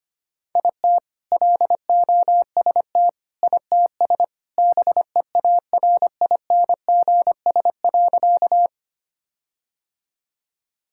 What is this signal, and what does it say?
Morse code, heard as a keyed tone, about 25 wpm, 700 Hz. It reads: IT LOST ITS BEARINGS.
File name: practice / 25wpm / All-Age Appropriate Jokes - 25wpm